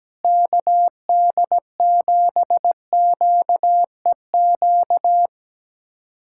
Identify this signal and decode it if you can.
Morse code, heard as a keyed tone, about 17 wpm, 700 Hz. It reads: KD7QEQ
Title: USA Callsign DE Callsign 17wpm